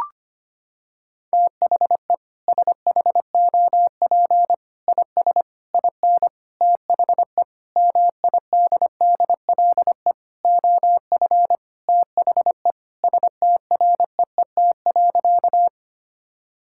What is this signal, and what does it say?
Morse code, heard as a keyed tone, about 25 wpm, 700 Hz. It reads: THE SHOP IS IN THE MIDDLE OF THE STREET.
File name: practice / 25wpm / Sentences with 4-Word Phrases - 25wpm